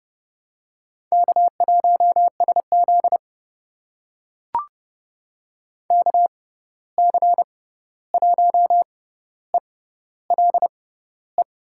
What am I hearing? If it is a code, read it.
Morse code, heard as a keyed tone, about 30 wpm, spaced as for 14 wpm, 700 Hz. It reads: K1SZ KC1ELE
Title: USA Call Signs - 6x Character Spacing - 30wpm